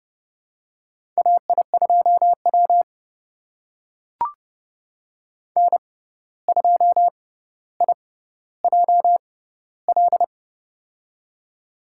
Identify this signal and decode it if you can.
Morse code, heard as a keyed tone, about 30 wpm, spaced as for 14 wpm, 700 Hz. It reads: AI2W N2IJL
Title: USA Call Signs - 6x Character Spacing - 30wpm